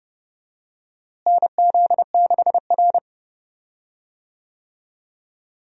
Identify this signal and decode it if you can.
Morse code, heard as a keyed tone, about 30 wpm, 700 Hz. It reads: NZ6R
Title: USA Call Signs - 4 Characters - Repeated 2x - 30wpm